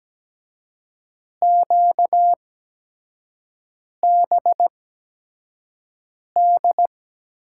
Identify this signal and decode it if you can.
Morse code, heard as a keyed tone, about 17 wpm, spaced as for 5 wpm, 700 Hz. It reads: QBD
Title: USA Call Signs - 6 Characters - 8x Character Spacing - 17wpm